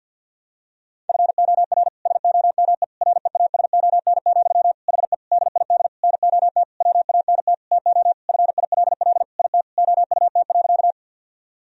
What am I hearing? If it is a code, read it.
Morse code, heard as a keyed tone, about 50 wpm, 700 Hz. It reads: FOR SOME REASON, HE DID NOT WANT TO FILL IT OUT.
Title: Sentences from Top 1000 Words - Review Part 3 - 50wpm